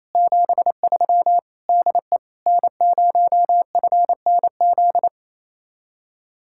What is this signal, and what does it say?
Morse code, heard as a keyed tone, about 36 wpm, 700 Hz. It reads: 73 DE N0FNZ